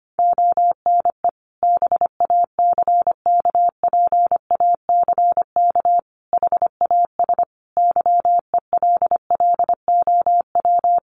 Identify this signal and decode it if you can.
Morse code, heard as a keyed tone, about 25 wpm, 700 Hz. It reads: ONE BACKPACK HAS YELLOW
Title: Additional Easy Phrases 25wpm